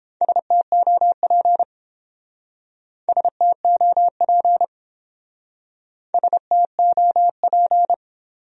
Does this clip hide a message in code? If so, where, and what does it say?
Yes: Morse code from 0.2 to 7.9 s, heard as a keyed tone, about 29 wpm, 700 Hz. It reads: STOP STOP STOP